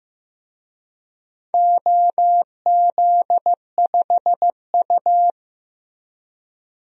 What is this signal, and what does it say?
Morse code, heard as a keyed tone, about 15 wpm, 700 Hz. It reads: OZ5U
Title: DXCC-100 Call Signs - 4 Characters - Repeated 3x - 15wpm